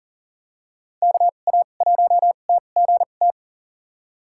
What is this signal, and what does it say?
Morse code, heard as a keyed tone, about 40 wpm, spaced as for 29 wpm, 700 Hz. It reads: KA1TGT
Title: USA Call Signs - 6 Characters - 2x Character Spacing - 40wpm